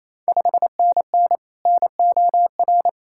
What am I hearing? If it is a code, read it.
Morse code, heard as a keyed tone, about 28 wpm, 700 Hz. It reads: 5NN NOR
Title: Ontario QSO Party Exchange 28wpm